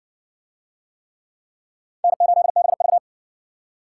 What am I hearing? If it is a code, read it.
Morse code, heard as a keyed tone, about 57 wpm, 700 Hz. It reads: N8BV